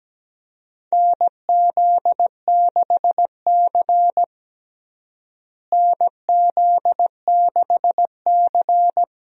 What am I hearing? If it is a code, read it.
Morse code, heard as a keyed tone, about 17 wpm, 700 Hz. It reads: NZ6C NZ6C
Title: USA Call Signs - 4 Characters - Repeated 2x - 17wpm